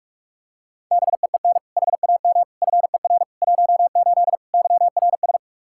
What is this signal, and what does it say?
Morse code, heard as a keyed tone, about 46 wpm, 700 Hz. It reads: BEEN HAM FER 18 YRS
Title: Practice QSO - Exchange 3 - More Info - 45wpm